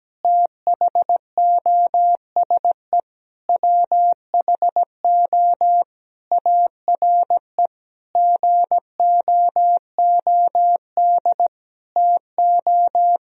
Morse code, heard as a keyed tone, about 17 wpm, 700 Hz. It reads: THOSE WHO ARE GOOD TO
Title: Sentences from Top 1000 Words - Review Part 1 - 17wpm